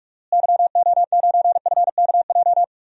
Morse code, heard as a keyed tone, about 43 wpm, 700 Hz. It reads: YO9FKJ
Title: DXCC-100 Call Signs - 6 Characters - Repeated 2x - 45wpm